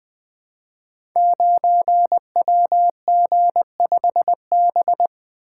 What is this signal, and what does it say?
Morse code, heard as a keyed tone, about 20 wpm, 700 Hz. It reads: 9WG5B
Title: Code Groups - 5 Letter:Numbers - 20wpm